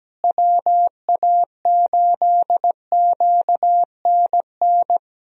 Morse code, heard as a keyed tone, about 17 wpm, 700 Hz. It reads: WA8QNN
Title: USA Call Signs - 6 Characters - Repeated 3x - 17wpm